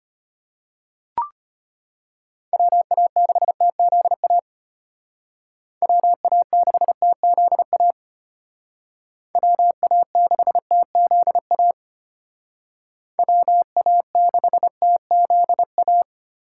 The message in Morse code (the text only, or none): WA6TZA WA6TZA WA6TZA WA6TZA